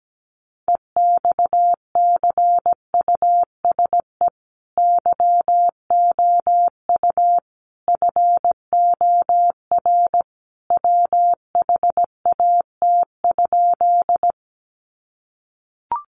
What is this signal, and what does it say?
Morse code, heard as a keyed tone, about 17 wpm, 700 Hz. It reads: EXCUSE YOU FOR WHAT?
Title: Sentences from Top 700 Words 17wpm